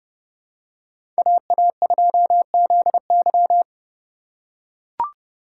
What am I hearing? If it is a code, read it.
Morse code, heard as a keyed tone, about 31 wpm, 700 Hz. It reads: AA2ZY
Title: USA Call Signs - 5 Characters - 4x Character Spacing - 30wpm